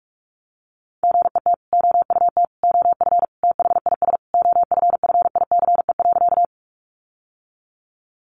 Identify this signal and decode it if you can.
Morse code, heard as a keyed tone, about 44 wpm, 700 Hz. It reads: GET OUT OF THIS OFFICE.
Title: Sentences from Top 1000 Words 45wpm